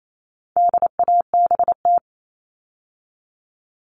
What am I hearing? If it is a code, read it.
Morse code, heard as a keyed tone, about 28 wpm, 700 Hz. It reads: DABT
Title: All Cut Numbers - 4-Digit 28wpm